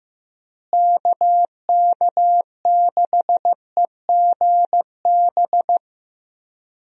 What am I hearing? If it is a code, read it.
Morse code, heard as a keyed tone, about 15 wpm, 700 Hz. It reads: KK6EGB
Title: USA Call Signs - 6 Characters - Repeated 3x - 15wpm